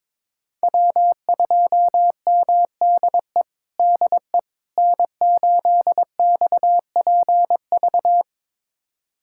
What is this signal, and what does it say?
Morse code, heard as a keyed tone, about 22 wpm, 700 Hz. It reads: W2MDE DE N8XPV
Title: USA Callsign DE Callsign - 5 Characters - 22wpm